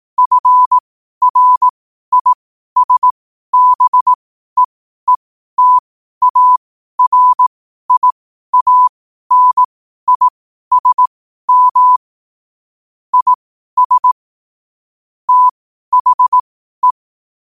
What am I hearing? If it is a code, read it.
Morse code, heard as a keyed tone, about 18 wpm, spaced as for 12 wpm, 1000 Hz. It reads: FRISBEETARIANISM IS THE
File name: qotdfiles / 12wpm